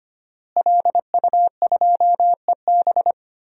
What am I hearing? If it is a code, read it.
Morse code, heard as a keyed tone, about 25 wpm, 700 Hz. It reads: LU2EB
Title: DXCC-100 Call Signs - 5 Characters - Repeated 3x - 25wpm